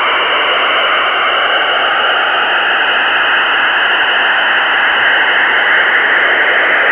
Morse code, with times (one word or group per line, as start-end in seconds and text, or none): none